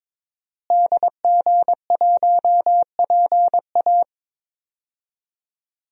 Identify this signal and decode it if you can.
Morse code, heard as a keyed tone, about 22 wpm, 700 Hz. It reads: DG1PA
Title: DXCC-100 Call Signs - 5 Characters - Repeated 2x - 22wpm